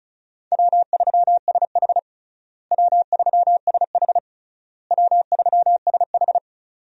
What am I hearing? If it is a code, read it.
Morse code, heard as a keyed tone, about 36 wpm, 700 Hz. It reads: W3SH W3SH W3SH
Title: USA Call Signs - 4 Characters - Repeated 3x - 35wpm